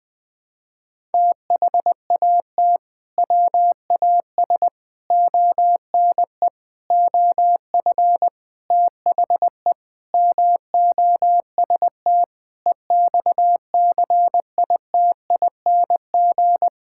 Morse code, heard as a keyed tone, about 20 wpm, 700 Hz. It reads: THAT WAS ONE OF THE MOST EXCITING
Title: Sentences with 5-Word Phrases - 20wpm